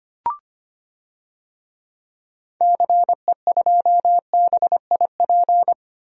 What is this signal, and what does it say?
Morse code, heard as a keyed tone, about 25 wpm, 700 Hz. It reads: CE2BIP
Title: DXCC-100 Call Signs - 6 Characters - 25wpm